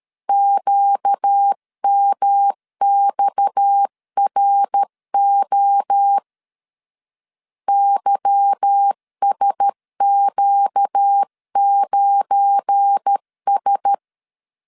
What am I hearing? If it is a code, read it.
Morse code, heard as a keyed tone, about 13 wpm, 800 Hz. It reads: QMXRO YSQ9S
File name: LEZIONI / 9 286 12wpm 2305